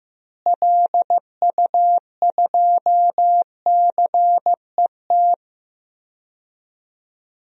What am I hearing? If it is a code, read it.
Morse code, heard as a keyed tone, about 15 wpm, 700 Hz. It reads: LU2CET
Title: DXCC-100 Call Signs - 6 Characters - Repeated 2x - 15wpm